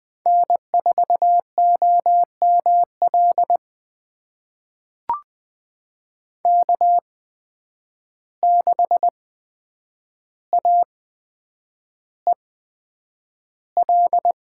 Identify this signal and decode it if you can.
Morse code, heard as a keyed tone, about 20 wpm, spaced as for 9 wpm, 700 Hz. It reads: N4OML K6AEL
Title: USA Call Signs - 5 Characters - 8x Character Spacing - 20wpm